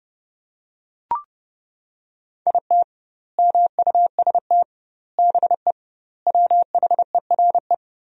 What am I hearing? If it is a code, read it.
Morse code, heard as a keyed tone, about 30 wpm, 700 Hz. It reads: IT MUST BE WHERE
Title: 2x Word-Spacing - Sentences from Top 200 Words - 30wpm